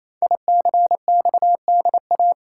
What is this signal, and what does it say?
Morse code, heard as a keyed tone, about 27 wpm, 700 Hz. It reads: ICXDA
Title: Code Groups - 5 Letter:Numbers - 28wpm